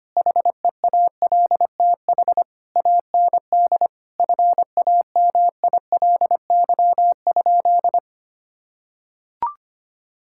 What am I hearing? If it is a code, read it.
Morse code, heard as a keyed tone, about 25 wpm, 700 Hz. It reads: HEALTH AND FAMILY?
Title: Sentences from Top 1600 Words 25wpm